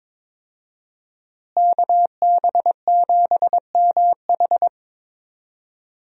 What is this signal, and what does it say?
Morse code, heard as a keyed tone, about 22 wpm, 700 Hz. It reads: KB7MH
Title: USA Call Signs - 5 Characters - Repeated 3x - 22wpm